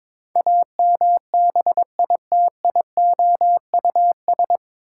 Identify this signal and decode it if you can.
Morse code, heard as a keyed tone, about 22 wpm, 700 Hz. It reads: AMBITIOUS
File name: 9 Letter Words - First 3 Letters Given - 22wpm